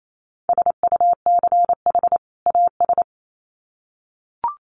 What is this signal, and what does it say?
Morse code, heard as a keyed tone, about 28 wpm, 700 Hz. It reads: SUCH AS
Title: Sets of 2 Words 28wpm